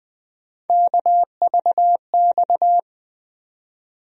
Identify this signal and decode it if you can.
Morse code, heard as a keyed tone, about 20 wpm, 700 Hz. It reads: KVX
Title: USA Call Sign Postfixes - 3 Letters 20wpm